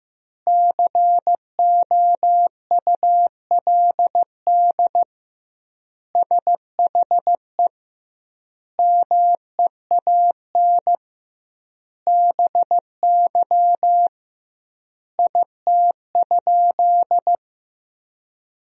Morse code, heard as a keyed tone, about 15 wpm, 700 Hz. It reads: COULD SHE MEAN BY IT?
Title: 2x Word-Spacing - Sentences from Top 200 Words - 15wpm